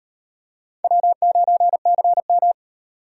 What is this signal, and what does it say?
Morse code, heard as a keyed tone, about 37 wpm, 700 Hz. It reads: W9CM